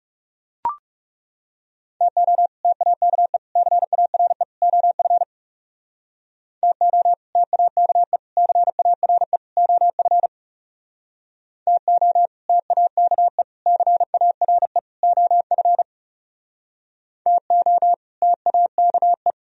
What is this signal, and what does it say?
Morse code, heard as a keyed tone, about 37 wpm, 700 Hz. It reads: TO TAKE CARE OF TO TAKE CARE OF TO TAKE CARE OF TO TAKE
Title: Speed-Racing - Sets of 4 Words 30wpm